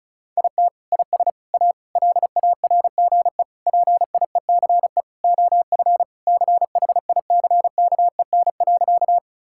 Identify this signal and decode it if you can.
Morse code, heard as a keyed tone, about 34 wpm, 700 Hz. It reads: IT IS A LARGE PIECE OF CHICKEN.
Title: Sentences from Top 1000 Words - Review Part 5 - 35wpm